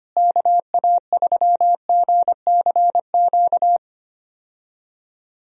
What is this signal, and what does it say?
Morse code, heard as a keyed tone, about 25 wpm, 700 Hz. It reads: KA3GCQ
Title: USA Call Signs - 6 Characters - Repeated 3x - 25wpm